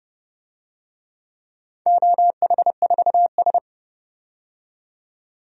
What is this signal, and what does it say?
Morse code, heard as a keyed tone, about 30 wpm, 700 Hz. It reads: OH4S
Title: DXCC-100 Call Signs - 4 Characters - 30wpm